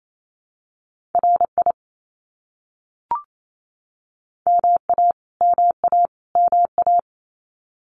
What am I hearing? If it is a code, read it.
Morse code, heard as a keyed tone, about 28 wpm, 700 Hz. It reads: RI MA MA MA
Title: US States Repeated 3x 28wpm